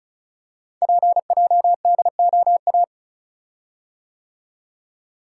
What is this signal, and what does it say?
Morse code, heard as a keyed tone, about 36 wpm, 700 Hz. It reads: PJDOA
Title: Code Groups - 5 Letters - 35wpm